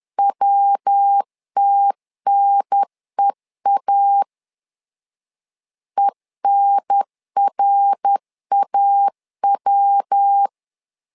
Morse code, heard as a keyed tone, about 10 wpm, 800 Hz. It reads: WTNEA ENRAW
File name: LEZIONI / 4 Q 10wpm 2210